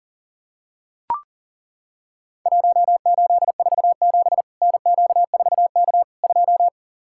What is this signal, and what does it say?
Morse code, heard as a keyed tone, about 40 wpm, 700 Hz. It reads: 1847 NQ4K 2
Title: Worked All Europe DX Contest - QTC Exchange - 40wpm